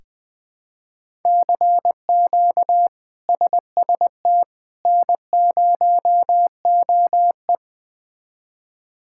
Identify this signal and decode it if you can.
Morse code, heard as a keyed tone, about 21 wpm, 700 Hz. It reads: CQ SST N0OE
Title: K1USN SST QSO Practice - 20wpm